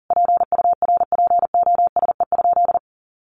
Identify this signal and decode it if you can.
Morse code, heard as a keyed tone, about 40 wpm, 700 Hz. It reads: PURPOSE?
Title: Sentences from Top 1500 Words 40wpm